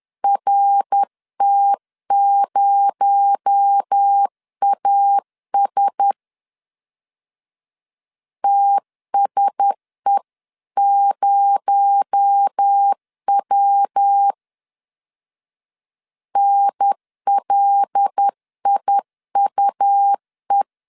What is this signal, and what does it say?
Morse code, heard as a keyed tone, about 10 wpm, 800 Hz. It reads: RT0AS TSE0W NLIUE
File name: LEZIONI / 6 PKX0 / 6 PKX0 10wpm 2707